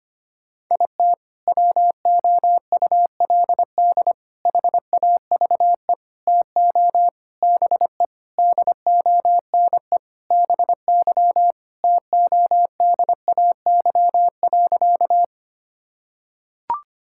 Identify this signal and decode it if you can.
Morse code, heard as a keyed tone, about 25 wpm, 700 Hz. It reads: IT WOULD HAVE TO BE DONE BY TODAY.
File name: practice / 25wpm / Sentences with 4-Word Phrases - 25wpm